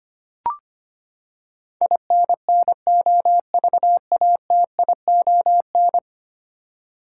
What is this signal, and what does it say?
Morse code, heard as a keyed tone, about 25 wpm, 700 Hz. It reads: INNOVATION
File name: Ten Letter Words - 25wpm